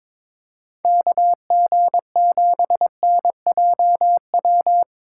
Morse code, heard as a keyed tone, about 22 wpm, 700 Hz. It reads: KG7NJW
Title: USA Call Signs - 2x Character Spacing - 22wpm